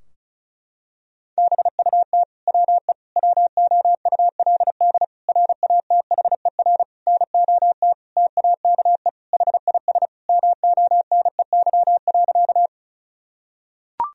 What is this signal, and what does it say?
Morse code, heard as a keyed tone, about 36 wpm, 700 Hz. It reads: BUT WE WOULD RATHER NOT TAKE HIS MONEY.
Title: Sentences from Top 1000 Words - Review Part 4 - 35wpm